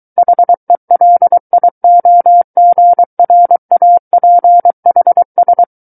Morse code, heard as a keyed tone, about 23 wpm, 700 Hz. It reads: HELIOGRAPHS
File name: Book One / 14-Chapter 14 - 23 WPM